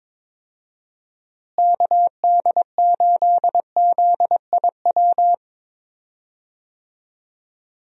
Morse code, heard as a keyed tone, about 22 wpm, 700 Hz. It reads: KD8ZIW
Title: USA Call Signs - 6 Characters - Repeated 2x - 22wpm